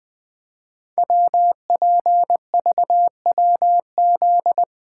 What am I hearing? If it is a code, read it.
Morse code, heard as a keyed tone, about 20 wpm, 700 Hz. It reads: WPVWZ
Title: Code Groups - 5 Letters - 20wpm